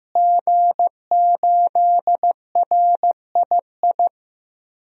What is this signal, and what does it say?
Morse code, heard as a keyed tone, about 15 wpm, 700 Hz. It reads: G8RII
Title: DXCC-100 Call Signs - 5 Characters - Repeated 3x - 15wpm